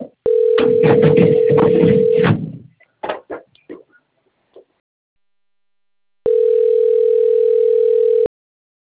Ringback tone (North America), 440 + 480 Hz.